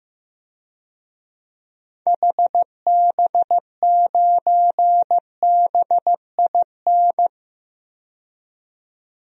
Morse code, heard as a keyed tone, about 15 wpm, 700 Hz. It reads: HB9BIN
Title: DXCC-100 Call Signs - 6 Characters - 15wpm